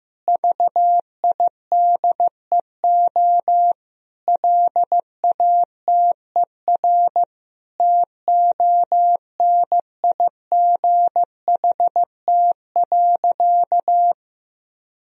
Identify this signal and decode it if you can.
Morse code, heard as a keyed tone, about 15 wpm, 700 Hz. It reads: VIDEO LATER TONIGHT.
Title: Sentences from Top 1000 Words - Review Part 4 - 15wpm